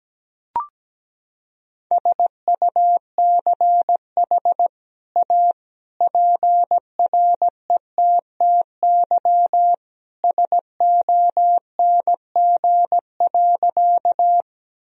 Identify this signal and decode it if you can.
Morse code, heard as a keyed tone, about 17 wpm, 700 Hz. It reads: SUCH A PRETTY SONG.
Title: Sentences from Top 1000 Words - Review Part 2 - 17wpm